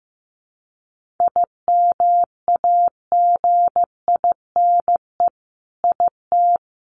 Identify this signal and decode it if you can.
Morse code, heard as a keyed tone, about 15 wpm, 700 Hz. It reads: IMAGINE IT
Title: Sentences from Top 1400 Words 15wpm